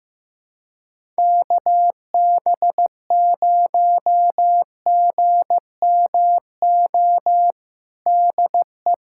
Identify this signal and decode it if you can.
Morse code, heard as a keyed tone, about 15 wpm, 700 Hz. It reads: KB0GMO DE
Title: USA Callsign DE Callsign 15wpm